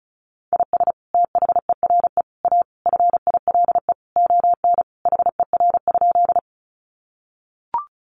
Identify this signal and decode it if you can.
Morse code, heard as a keyed tone, about 34 wpm, 700 Hz. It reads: IS THERE A FILE ON HER?
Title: Sentences from Top 1200 Words 35wpm